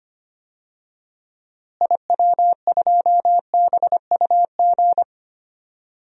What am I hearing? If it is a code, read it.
Morse code, heard as a keyed tone, about 25 wpm, 700 Hz. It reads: IW2BUG
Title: DXCC-100 Call Signs - 6 Characters - Repeated 2x - 25wpm